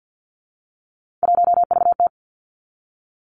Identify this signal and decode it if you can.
Morse code, heard as a keyed tone, about 50 wpm, 700 Hz. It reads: 1VT
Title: Three Challenging Letter-Number Combinations 50wpm